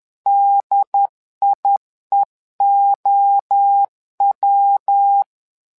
Morse code, heard as a keyed tone, about 10 wpm, 800 Hz. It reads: DIEOW
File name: LEZIONI / 3 MWRO / 3 MWRO 10wpm 2148